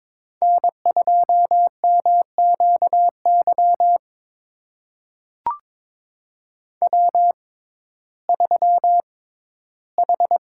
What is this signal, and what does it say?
Morse code, heard as a keyed tone, about 22 wpm, spaced as for 14 wpm, 700 Hz. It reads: N2MQY W3H
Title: USA Call Signs - 5 Characters - 6x Character Spacing - 22wpm